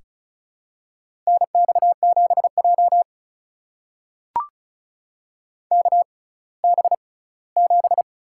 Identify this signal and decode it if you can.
Morse code, heard as a keyed tone, about 34 wpm, spaced as for 20 wpm, 700 Hz. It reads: NX7J KB7
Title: USA Call Signs - 6x Character Spacing - 35wpm